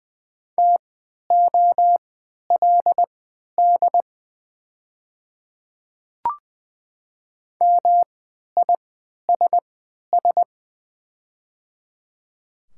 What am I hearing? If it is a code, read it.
Morse code, heard as a keyed tone, about 20 wpm, spaced as for 11 wpm, 700 Hz. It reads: TOLD MISS
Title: 3x Character Spacing - Top 500 Words - 20wpm